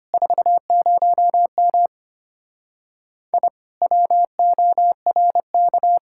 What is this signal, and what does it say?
Morse code, heard as a keyed tone, about 33 wpm, 700 Hz. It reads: 40M I WORK